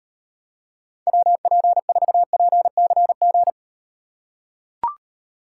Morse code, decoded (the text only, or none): WP4PCG